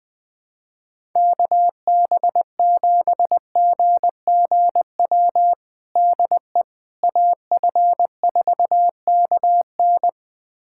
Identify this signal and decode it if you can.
Morse code, heard as a keyed tone, about 20 wpm, 700 Hz. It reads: KB7GGW DE AF4KN
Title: USA Callsign DE Callsign 20wpm